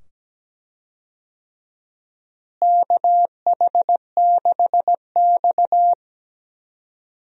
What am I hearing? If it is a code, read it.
Morse code, heard as a keyed tone, about 17 wpm, 700 Hz. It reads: KH6X
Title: DXCC-100 Call Signs - 4 Characters - Repeated 2x - 17wpm